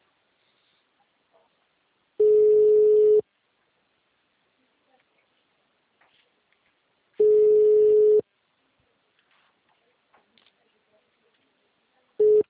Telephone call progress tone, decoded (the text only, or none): ringback tone